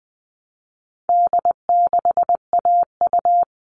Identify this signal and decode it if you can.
Morse code, heard as a keyed tone, about 20 wpm, 700 Hz. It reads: D6AU
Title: All Cut Numbers - 4-Digit 20wpm